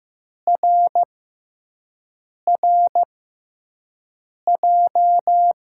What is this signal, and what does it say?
Morse code, heard as a keyed tone, about 15 wpm, spaced as for 5 wpm, 700 Hz. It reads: RRJ
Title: USA Call Signs - 6x Character Spacing - 15wpm